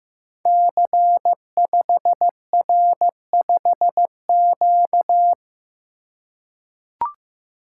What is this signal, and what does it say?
Morse code, heard as a keyed tone, about 15 wpm, 700 Hz. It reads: C5R5Q
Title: Code Groups - 5 Letter:Numbers - 15wpm